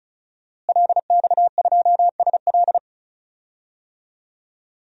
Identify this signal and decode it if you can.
Morse code, heard as a keyed tone, about 34 wpm, 700 Hz. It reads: LX2SL
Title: DXCC-100 Call Signs - 5 Characters - Repeated 3x - 35wpm